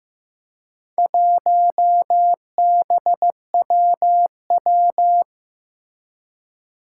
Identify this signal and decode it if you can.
Morse code, heard as a keyed tone, about 15 wpm, 700 Hz. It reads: 1BWW
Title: Four Challenging Letter-Number Combinations 15wpm